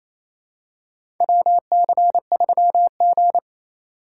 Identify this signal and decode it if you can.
Morse code, heard as a keyed tone, about 28 wpm, 700 Hz. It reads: WC3G
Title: USA Call Signs - 4 Characters - 8x Character Spacing - 28wpm